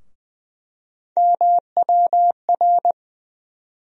Morse code, heard as a keyed tone, about 20 wpm, 700 Hz. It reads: MWR
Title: USA Call Sign Postfixes - 3 Letters 20wpm